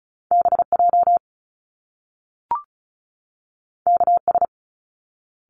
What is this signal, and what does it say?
Morse code, heard as a keyed tone, about 36 wpm, 700 Hz. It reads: BJ KS
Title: Top Two Letter Combinations in Words 35wpm